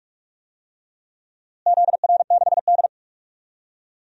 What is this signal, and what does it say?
Morse code, heard as a keyed tone, about 44 wpm, 700 Hz. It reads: ZR6D
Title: DXCC-100 Call Signs - 4 Characters - Repeated 2x - 45wpm